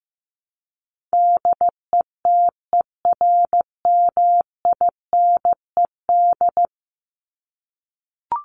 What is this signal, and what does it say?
Morse code, heard as a keyed tone, about 15 wpm, 700 Hz. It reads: DETERMINED
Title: ED Suffix - Top 100 Words with Base Words 15wpm